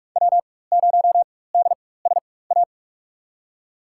Morse code, heard as a keyed tone, about 46 wpm, spaced as for 21 wpm, 690 Hz. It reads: W0DSA
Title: USA Call Signs - 4x Character Spacing - 45wpm